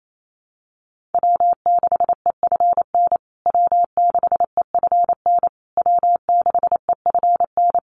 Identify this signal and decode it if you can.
Morse code, heard as a keyed tone, about 28 wpm, 700 Hz. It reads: W6EFN W6EFN W6EFN
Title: USA Call Signs Repeated 3x 28wpm